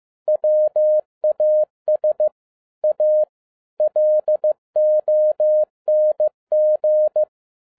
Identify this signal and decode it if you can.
Morse code, heard as a keyed tone, about 15 wpm, 610 Hz. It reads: WAS A LONG
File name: 2011-07-mp3 / n12 T8XO 15wpm